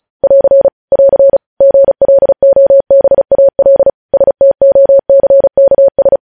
Morse code, heard as a keyed tone, about 36 wpm, 550 Hz. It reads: <AR> <AR> GLOBAL STOCKS
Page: news headlines in morse code at 35 wpm